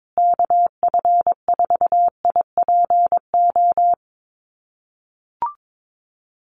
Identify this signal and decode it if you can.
Morse code, heard as a keyed tone, about 22 wpm, 700 Hz. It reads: KF4IPO
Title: USA Call Signs Repeated 2x 22wpm